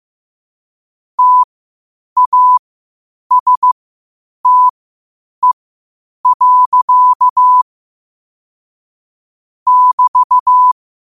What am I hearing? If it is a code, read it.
Morse code, heard as a keyed tone, about 15 wpm, spaced as for 8 wpm, 1000 Hz. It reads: TASTE. <BT>
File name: qotdfiles / 8wpm